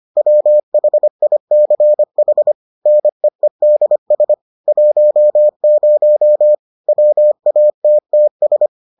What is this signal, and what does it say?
Morse code, heard as a keyed tone, about 25 wpm, 600 Hz. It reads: WHICH NEEDS 10 WATTS